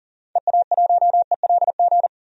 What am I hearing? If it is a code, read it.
Morse code, heard as a keyed tone, about 40 wpm, 700 Hz. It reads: EA1ELG
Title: DXCC-100 Call Signs - 6 Characters - 40wpm